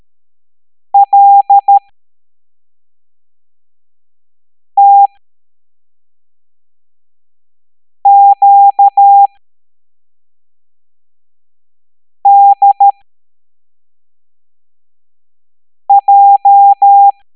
Morse code, heard as a keyed tone, about 13 wpm, spaced as for 3 wpm, 800 Hz. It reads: LTQDJ